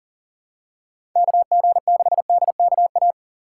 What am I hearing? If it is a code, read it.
Morse code, heard as a keyed tone, about 40 wpm, 700 Hz. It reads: KG6DKA
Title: USA Call Signs - 6 Characters - 4x Character Spacing - 40wpm